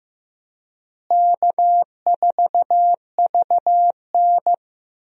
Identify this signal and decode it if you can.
Morse code, heard as a keyed tone, about 15 wpm, 700 Hz. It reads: K4VN
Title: USA Call Signs - 4 Characters - Repeated 2x - 15wpm